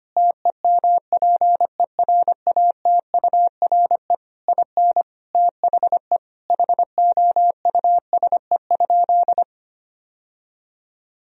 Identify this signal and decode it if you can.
Morse code, heard as a keyed tone, about 25 wpm, 700 Hz. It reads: TEMPERATURE IN THE HOUSE?
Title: Sentences from Top 1000 Words - Review Part 4 - 25wpm